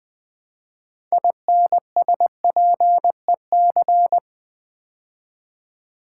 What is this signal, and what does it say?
Morse code, heard as a keyed tone, about 20 wpm, 700 Hz. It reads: INSPEC
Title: Build A Word - 3 Letters At A Time - 9-Letter Words - 20wpm